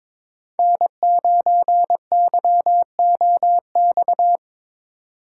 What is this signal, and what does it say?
Morse code, heard as a keyed tone, about 21 wpm, 700 Hz. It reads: N9YOX
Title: USA Call Signs - 6x Character Spacing - 22wpm